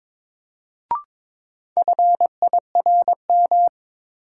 Morse code, heard as a keyed tone, about 22 wpm, 700 Hz. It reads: FIRM
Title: Sentences from Top 1400 Words - New Word Drill - 22wpm